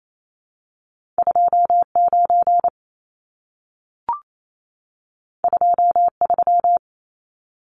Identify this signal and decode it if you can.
Morse code, heard as a keyed tone, about 28 wpm, 700 Hz. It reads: 29 23
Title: Numbers - 2-Digit 28wpm